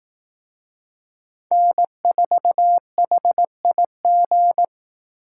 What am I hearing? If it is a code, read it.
Morse code, heard as a keyed tone, about 18 wpm, 700 Hz. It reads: N4HIG
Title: Speed Racing - USA Call Signs 15wpm